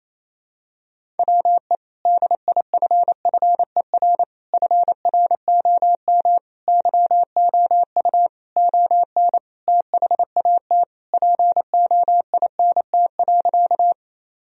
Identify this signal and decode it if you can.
Morse code, heard as a keyed tone, about 28 wpm, 700 Hz. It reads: WE DIFFER FROM YOU ON THAT POINT.
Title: Sentences from Top 1000 Words - Review Part 1 - 28wpm